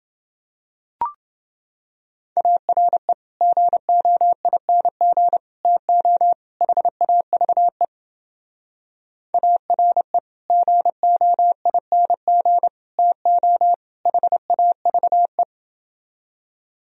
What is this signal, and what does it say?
Morse code, heard as a keyed tone, about 29 wpm, 700 Hz. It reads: ARE GOING TO HAVE ARE GOING TO HAVE